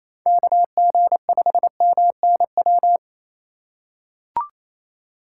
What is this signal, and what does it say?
Morse code, heard as a keyed tone, about 27 wpm, 700 Hz. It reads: KG5MNW
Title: USA Call Signs - 3x Character Spacing - 28wpm